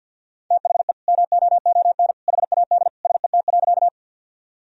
Morse code, heard as a keyed tone, about 52 wpm, 700 Hz. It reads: THE MOON HAD SET.
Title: Sentences from Top 1000 Words - Review Part 3 - 50wpm